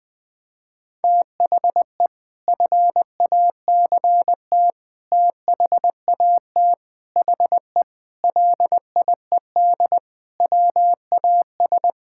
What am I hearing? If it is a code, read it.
Morse code, heard as a keyed tone, about 21 wpm, 700 Hz. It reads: THE FACT THAT HE LIED WAS
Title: Sentences with 4-Word Phrases - 20wpm